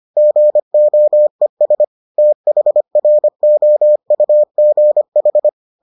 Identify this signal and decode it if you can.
Morse code, heard as a keyed tone, about 25 wpm, 600 Hz. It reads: GOES THROUGH